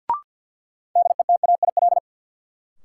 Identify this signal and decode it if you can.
Morse code, heard as a keyed tone, about 47 wpm, 700 Hz. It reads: DETAIL
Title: Sentences from Top 1500 Words - New Word Drill Alt - 50wpm